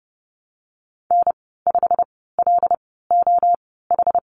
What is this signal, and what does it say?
Morse code, heard as a keyed tone, about 30 wpm, spaced as for 17 wpm, 700 Hz. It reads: N5LOH
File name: course / 30:10wpm / 116 Lesson 19g - Letter w - Call Signs 30wpm